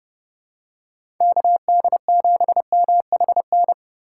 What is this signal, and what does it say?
Morse code, heard as a keyed tone, about 30 wpm, 700 Hz. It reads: KD7MHN
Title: USA Call Signs - 2x Character Spacing - 30wpm